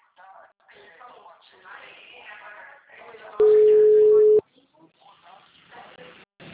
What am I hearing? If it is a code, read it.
Ringback tone (Europe, 425 Hz), with speech.